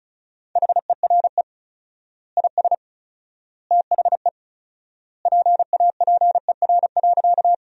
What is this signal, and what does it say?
Morse code, heard as a keyed tone, about 34 wpm, 700 Hz. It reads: HERE IS THE PAPER.
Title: 4x Word-Spacing - Sentences from Top 300 Words - 35wpm